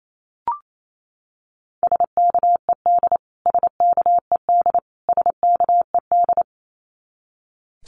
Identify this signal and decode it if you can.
Morse code, heard as a keyed tone, about 28 wpm, 700 Hz. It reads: SKED SKED SKED
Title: Abbreviations Repeated 3x - 28wpm